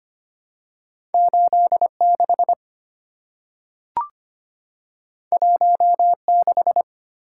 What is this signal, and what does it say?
Morse code, heard as a keyed tone, about 25 wpm, 700 Hz. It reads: 86 16
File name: Two Letter-Number Combinations - 25wpm